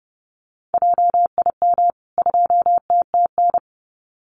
Morse code, heard as a keyed tone, about 30 wpm, 700 Hz. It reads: JIM 2TTN
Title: CWT Exchange 30wpm